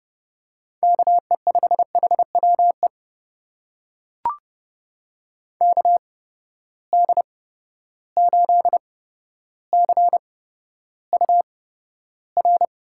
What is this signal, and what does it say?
Morse code, heard as a keyed tone, about 30 wpm, spaced as for 13 wpm, 700 Hz. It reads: KE5HWE KD8CUR
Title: USA Call Signs - 6 Characters - 8x Character Spacing - 30wpm